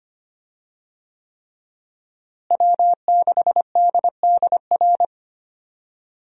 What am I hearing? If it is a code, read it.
Morse code, heard as a keyed tone, about 25 wpm, 700 Hz. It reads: W6DDR
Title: USA Call Signs - 5 Characters - Repeated 3x - 25wpm